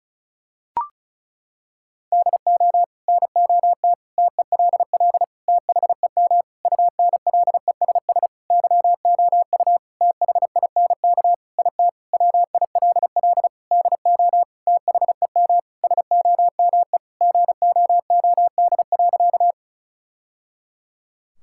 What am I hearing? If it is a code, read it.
Morse code, heard as a keyed tone, about 36 wpm, 700 Hz. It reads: DO NOT TELL THEM UNLESS YOU THINK IT WILL DO THEM SOME GOOD.
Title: Sentences from Top 1000 Words - Review Part 6 - 35wpm